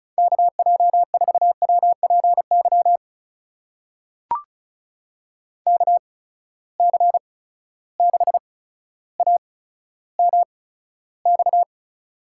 Morse code, heard as a keyed tone, about 36 wpm, spaced as for 15 wpm, 700 Hz. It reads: KJ4WPY KC6AMX
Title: USA Call Signs - 8x Character Spacing - 35wpm